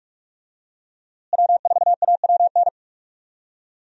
Morse code, heard as a keyed tone, about 46 wpm, 700 Hz. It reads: W4AWN
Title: USA Call Signs - 5 Characters - Repeated 3x - 45wpm